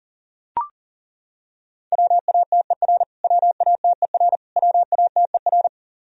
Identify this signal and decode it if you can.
Morse code, heard as a keyed tone, about 40 wpm, 700 Hz. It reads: WATER WATER WATER
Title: Top 100 Words Repeated 40wpm